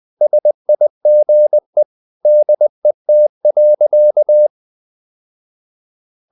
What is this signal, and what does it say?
Morse code, heard as a keyed tone, about 20 wpm, 600 Hz. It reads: SIGE DET.
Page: Ferie 20 wpm